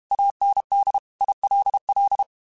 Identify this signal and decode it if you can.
Morse code, heard as a keyed tone, about 32 wpm, 800 Hz. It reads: AND ILL
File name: QSOs / QSOs-mp3s / kc7sac@30